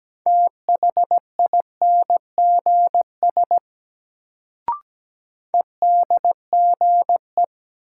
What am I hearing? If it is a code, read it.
Morse code, heard as a keyed tone, about 17 wpm, 700 Hz. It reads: THINGS EDGE
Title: Sets of 3 Words - New Word Drill - 17wpm